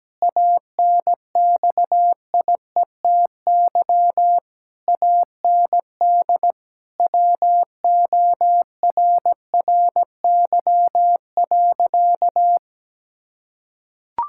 Morse code, heard as a keyed tone, about 17 wpm, 700 Hz. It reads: ANXIETY AND WORRY.
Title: Pangram Challenge v2 - 17wpm